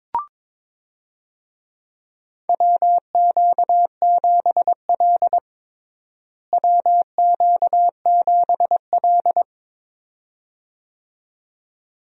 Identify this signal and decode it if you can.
Morse code, heard as a keyed tone, about 21 wpm, 700 Hz. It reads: WQ7L WQ7L
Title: USA Call Signs - 4 Characters - Repeated 2x - 22wpm